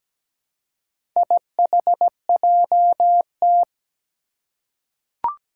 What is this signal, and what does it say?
Morse code, heard as a keyed tone, about 17 wpm, 700 Hz. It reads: IHJT
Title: Four Challenging Letter-Number Combinations 17wpm